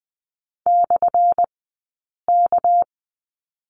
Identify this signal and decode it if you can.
Morse code, heard as a keyed tone, about 20 wpm, spaced as for 8 wpm, 700 Hz. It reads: /K
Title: 197 Lesson 32d - Letter k - Two Characters 20wpm